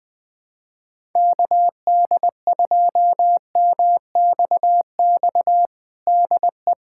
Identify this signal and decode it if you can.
Morse code, heard as a keyed tone, about 20 wpm, 700 Hz. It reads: KD2MXX DE
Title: USA Callsign DE Callsign 20wpm